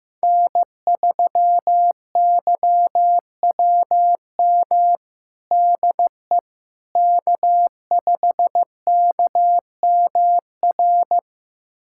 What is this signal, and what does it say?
Morse code, heard as a keyed tone, about 15 wpm, 700 Hz. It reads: N3YWM DE K5KMR
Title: USA Callsign DE Callsign - 5 Characters - 15wpm